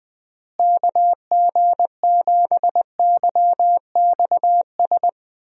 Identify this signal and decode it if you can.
Morse code, heard as a keyed tone, about 20 wpm, 700 Hz. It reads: KG7YXS
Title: USA Call Signs - 6 Characters - Repeated 3x - 20wpm